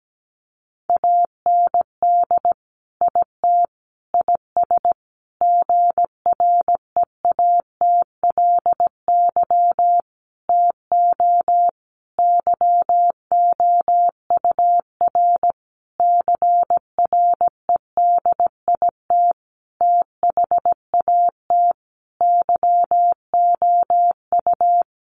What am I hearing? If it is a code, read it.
Morse code, heard as a keyed tone, about 17 wpm, 700 Hz. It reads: AND IT IS GREATLY TO YOUR CREDIT THAT YOU
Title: Sentences from Top 1400 Words 17wpm